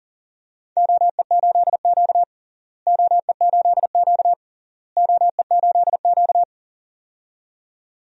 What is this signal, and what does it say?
Morse code, heard as a keyed tone, about 40 wpm, 700 Hz. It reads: OE8Q OE8Q OE8Q
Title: DXCC-100 Call Signs - 4 Characters - Repeated 3x - 40wpm